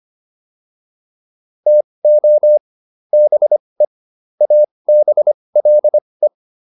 Morse code, heard as a keyed tone, about 25 wpm, spaced as for 20 wpm, 600 Hz. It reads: TO BE ABLE